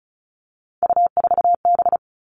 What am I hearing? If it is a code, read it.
Morse code, heard as a keyed tone, about 36 wpm, 700 Hz. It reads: U4B
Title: Three Challenging Letter-Number Combinations 35wpm